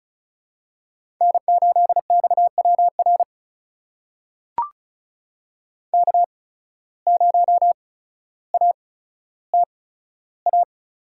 Morse code, heard as a keyed tone, about 36 wpm, spaced as for 15 wpm, 700 Hz. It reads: N8XWR K0ATA
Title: USA Call Signs - 5 Characters - 8x Character Spacing - 35wpm